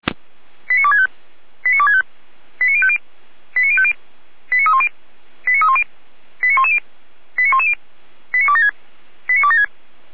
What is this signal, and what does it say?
Five-tone selective call (ZVEI), heard as quick runs of five tones, tones of about 70 ms. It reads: 89265 89265 89050 89050 89310 89310 89110 89110 89276 89276